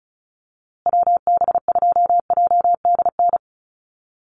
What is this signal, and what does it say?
Morse code, heard as a keyed tone, about 34 wpm, 700 Hz. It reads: WB2JDN
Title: USA Call Signs Repeated 2x 35wpm